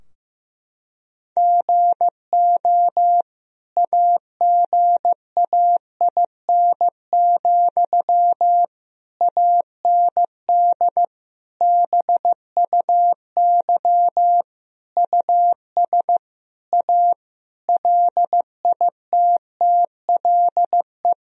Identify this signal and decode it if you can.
Morse code, heard as a keyed tone, about 15 wpm, 700 Hz. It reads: GO AGAIN, AND BUY US A LITTLE
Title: Sentences from Top 1000 Words - Review Part 4 - 15wpm